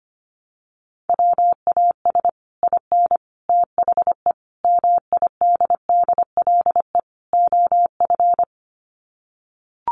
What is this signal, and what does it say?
Morse code, heard as a keyed tone, about 25 wpm, 700 Hz. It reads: WAS IN THE MIDDLE OF
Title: Sets of 5 Words Encore 25wpm